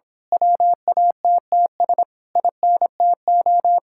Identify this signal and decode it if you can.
Morse code, heard as a keyed tone, about 26 wpm, 700 Hz. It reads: WATTS INTO
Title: Speed-Racing Common QSO Phrases Encore - 22wpm